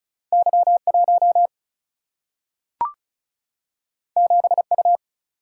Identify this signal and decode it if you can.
Morse code, heard as a keyed tone, about 36 wpm, 700 Hz. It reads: Y 1 7 U
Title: Single-Letter Single-Number 35wpm